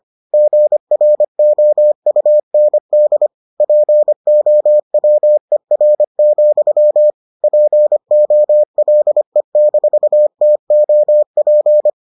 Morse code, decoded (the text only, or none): GROUND POWER, POLE-TOP